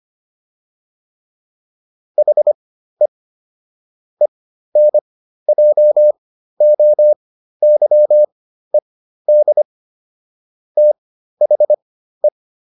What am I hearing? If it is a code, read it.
Morse code, heard as a keyed tone, about 25 wpm, spaced as for 13 wpm, 600 Hz. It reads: HE ENJOYED THE